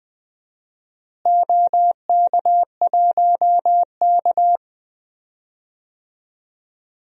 Morse code, heard as a keyed tone, about 20 wpm, 700 Hz. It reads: OK1K
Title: DXCC-100 Call Signs - 4 Characters - Repeated 3x - 20wpm